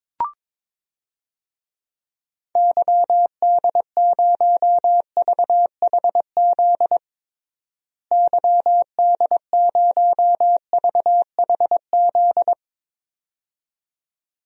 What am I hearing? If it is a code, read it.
Morse code, heard as a keyed tone, about 22 wpm, 700 Hz. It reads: YD0VHZ YD0VHZ
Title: DXCC-100 Call Signs - 6 Characters - Repeated 2x - 22wpm